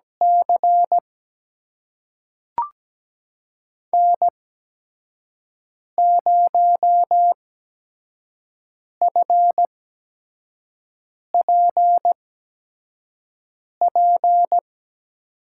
Morse code, heard as a keyed tone, about 17 wpm, spaced as for 5 wpm, 700 Hz. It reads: C N0FPP
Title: USA Call Signs - 5 Characters - 8x Character Spacing - 17wpm